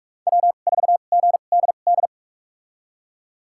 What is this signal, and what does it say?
Morse code, heard as a keyed tone, about 41 wpm, spaced as for 31 wpm, 700 Hz. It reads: W4GDD
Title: USA Call Signs - 5 Characters - 2x Character Spacing - 45wpm